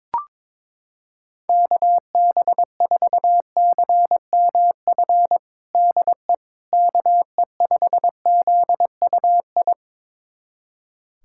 Morse code, heard as a keyed tone, about 22 wpm, 700 Hz. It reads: KB4CMF DE KE5ZUI
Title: USA Callsign DE Callsign 22wpm